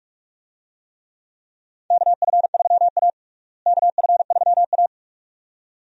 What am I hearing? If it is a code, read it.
Morse code, heard as a keyed tone, about 46 wpm, 700 Hz. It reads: KF3A KF3A
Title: USA Call Signs - 4 Characters - Repeated 2x - 45wpm